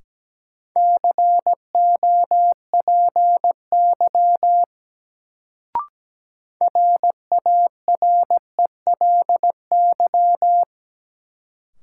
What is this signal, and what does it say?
Morse code, heard as a keyed tone, about 17 wpm, 700 Hz. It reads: COPY RARELY
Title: Sentences from Top 1400 Words - New Word Drill - 17wpm